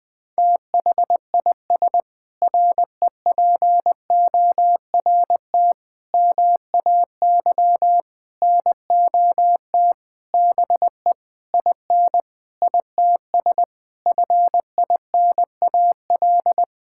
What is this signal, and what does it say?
Morse code, heard as a keyed tone, about 20 wpm, 700 Hz. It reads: THIS REPORT MAY NOT BE IN ITS FINAL
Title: Sentences with 7-Word Phrases - 20wpm